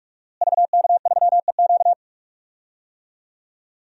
Morse code, heard as a keyed tone, about 43 wpm, 700 Hz. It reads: VK3EQ
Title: DXCC-100 Call Signs - 5 Characters - Repeated 3x - 45wpm